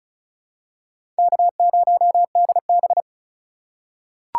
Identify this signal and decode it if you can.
Morse code, heard as a keyed tone, about 34 wpm, 700 Hz. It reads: K0DB
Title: USA Call Signs - 4 Characters - 4x Character Spacing - 35wpm